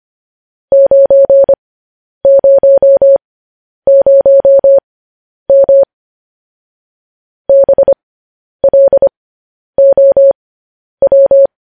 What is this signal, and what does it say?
Morse code, heard as a keyed tone, about 25 wpm, spaced as for 10 wpm, 550 Hz. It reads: 900M BLOW
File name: podcast / News headlines-1x-25x10